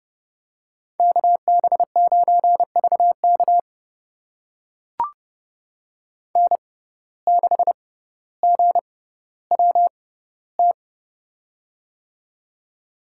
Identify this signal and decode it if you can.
Morse code, heard as a keyed tone, about 30 wpm, spaced as for 15 wpm, 700 Hz. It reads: KB9VK N6GWT
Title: USA Call Signs - 5 Characters - 6x Character Spacing - 30wpm